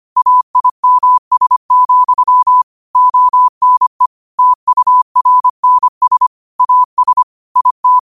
Morse code, heard as a keyed tone, about 25 wpm, 1000 Hz. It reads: AIMS, ONE TURNS AS IT
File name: 25wpm